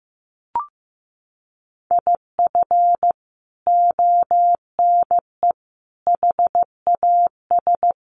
Morse code, heard as a keyed tone, about 15 wpm, 700 Hz. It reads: IF ONE HAS
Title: Sentences from Top 1400 Words 15wpm